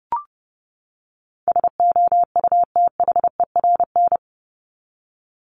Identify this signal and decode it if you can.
Morse code, heard as a keyed tone, about 30 wpm, 700 Hz. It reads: SOUTHERN
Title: Top 1751-2000 Words 30wpm